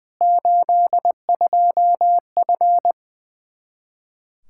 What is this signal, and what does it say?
Morse code, heard as a keyed tone, about 20 wpm, 700 Hz. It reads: 82F
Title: Practice QSO - Exchange 2 - Info - 20wpm